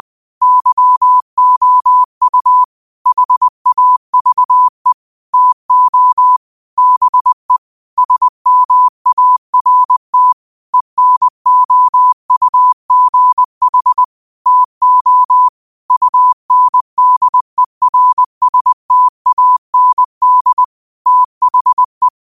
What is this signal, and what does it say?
Morse code, heard as a keyed tone, about 20 wpm, 1000 Hz. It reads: YOU HAVE TO BE SMART ENOUGH TO UNDERSTAND THE
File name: qotdfiles / 20wpm